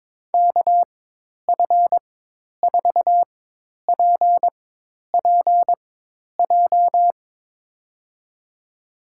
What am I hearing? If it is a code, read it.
Morse code, heard as a keyed tone, about 22 wpm, spaced as for 10 wpm, 700 Hz. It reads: KF4PPJ
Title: USA Call Signs - 6 Characters - 4x Character Spacing - 22wpm